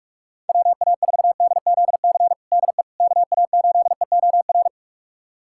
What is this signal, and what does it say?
Morse code, heard as a keyed tone, about 42 wpm, 690 Hz. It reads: WA4DZC DE KA8EOR
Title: USA Callsign DE Callsign - 6 Characters - 45wpm